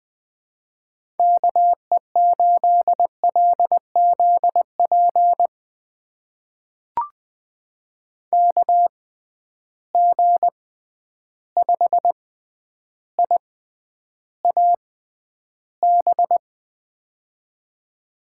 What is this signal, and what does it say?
Morse code, heard as a keyed tone, about 20 wpm, spaced as for 10 wpm, 700 Hz. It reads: KE8LZP KG5IAB
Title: USA Call Signs - 6 Characters - 6x Character Spacing - 20wpm